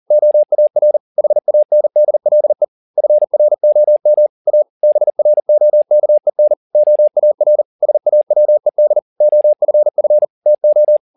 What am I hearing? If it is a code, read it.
Morse code, heard as a keyed tone, about 40 wpm, 600 Hz. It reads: OAR HANDLE FROM A BROKEN OAR SAWED OFF TO
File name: the old man and the sea 2 40wpm